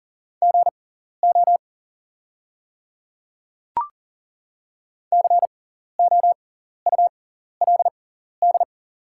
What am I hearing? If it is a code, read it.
Morse code, heard as a keyed tone, about 40 wpm, spaced as for 14 wpm, 700 Hz. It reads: GO COULD
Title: 6x Character Spacing - Top 100 Words - 40wpm